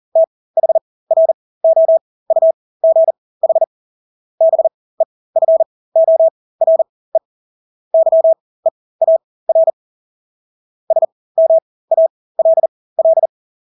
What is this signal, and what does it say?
Morse code, heard as a keyed tone, about 40 wpm, spaced as for 20 wpm, 650 Hz. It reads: THROUGH BEFORE YEAR SMALL
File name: Engelske ord 40-20 wpm